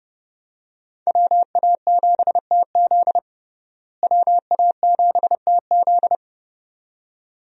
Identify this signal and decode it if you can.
Morse code, heard as a keyed tone, about 29 wpm, 700 Hz. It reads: WA7TZ WA7TZ
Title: USA Call Signs - 5 Characters - Repeated 2x - 30wpm